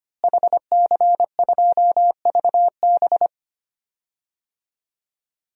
Morse code, heard as a keyed tone, about 25 wpm, 700 Hz. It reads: HC2VB
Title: DXCC-100 Call Signs - 5 Characters - Repeated 2x - 25wpm